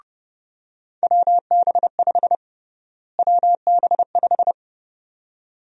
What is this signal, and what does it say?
Morse code, heard as a keyed tone, about 29 wpm, 700 Hz. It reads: WB5 WB5
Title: USA Call Sign Prefixes - Repeated 2x - 30wpm